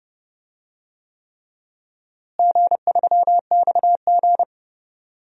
Morse code, heard as a keyed tone, about 30 wpm, 700 Hz. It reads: G3XG
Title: DXCC-100 Call Signs - 4 Characters - Repeated 3x - 30wpm